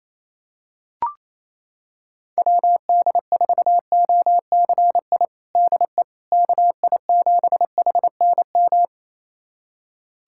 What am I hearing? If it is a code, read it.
Morse code, heard as a keyed tone, about 28 wpm, 700 Hz. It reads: WD4OCI DE KI7HNM
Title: USA Callsign DE Callsign - 6 Characters - 28wpm